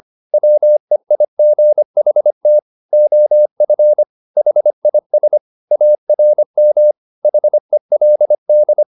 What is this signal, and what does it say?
Morse code, heard as a keyed tone, about 25 wpm, 600 Hz. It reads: WEIGHT OF HIS ARM HELD
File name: the old man and the sea part 1 25wpm